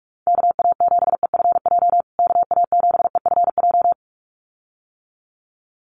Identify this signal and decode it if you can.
Morse code, heard as a keyed tone, about 44 wpm, 690 Hz. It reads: KA7EFJ KA7EFJ
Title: USA Call Signs Repeated 2x 45wpm